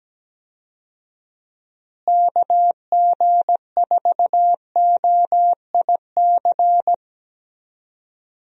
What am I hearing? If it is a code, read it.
Morse code, heard as a keyed tone, about 17 wpm, 700 Hz. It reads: KG4OIC
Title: USA Call Signs - 6 Characters - Repeated 2x - 17wpm